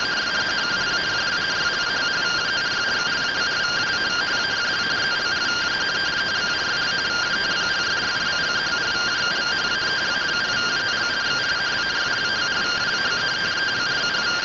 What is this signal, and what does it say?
Two-tone teleprinter signal, about 170 Hz shift.